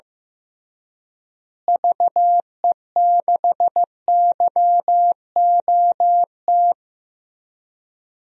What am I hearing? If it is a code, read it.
Morse code, heard as a keyed tone, about 15 wpm, 700 Hz. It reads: VE6YOT
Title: DXCC-100 Call Signs - 6 Characters - Repeated 3x - 15wpm